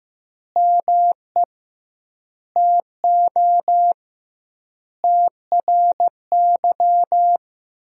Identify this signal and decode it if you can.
Morse code, heard as a keyed tone, about 15 wpm, 700 Hz. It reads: ME TO TRY
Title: 2x Word-Spacing - Sentences from Top 400 Words - 15wpm